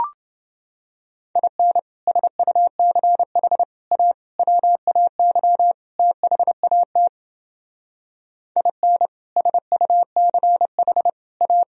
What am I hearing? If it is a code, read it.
Morse code, heard as a keyed tone, about 28 wpm, 700 Hz. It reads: IN SUCH A WAY THAT IN SUCH A